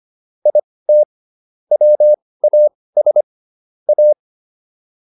Morse code, heard as a keyed tone, about 25 wpm, spaced as for 18 wpm, 600 Hz. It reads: IT WAS A